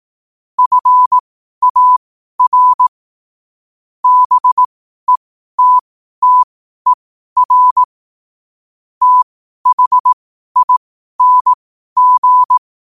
Morse code, heard as a keyed tone, about 18 wpm, spaced as for 12 wpm, 1000 Hz. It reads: FAR BETTER THING
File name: qotdfiles / 12wpm